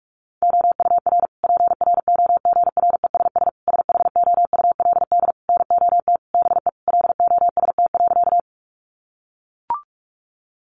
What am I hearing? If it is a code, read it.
Morse code, heard as a keyed tone, about 44 wpm, 700 Hz. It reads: OUR PROGRESS SHOULD NOT BE LOST.
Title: Sentences from Top 1400 Words 45wpm